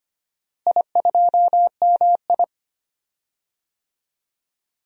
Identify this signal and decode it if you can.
Morse code, heard as a keyed tone, about 25 wpm, 700 Hz. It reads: I2MI
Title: DXCC-100 Call Signs - 4 Characters - Repeated 2x - 25wpm